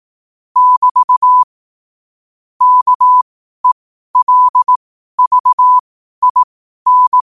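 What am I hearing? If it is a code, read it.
Morse code, heard as a keyed tone, about 18 wpm, spaced as for 12 wpm, 1000 Hz. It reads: <BT> KELVIN
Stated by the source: Morse code quote of the day at 12 wpm.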